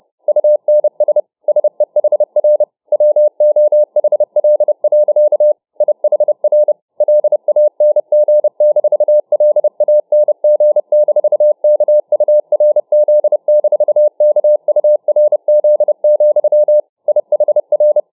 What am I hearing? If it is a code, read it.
Morse code, heard as a keyed tone, about 31 wpm, 600 Hz. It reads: UNS SEHR WOHL. IHR LANG-LANG-KURZ-KURZ, IHR